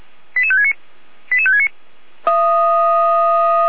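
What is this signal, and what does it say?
Five-tone selective call (ZVEI), heard as quick runs of five tones, tones of about 70 ms. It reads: 88569 88569
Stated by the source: ZVEI five-tone selective call 88485 received via SDR.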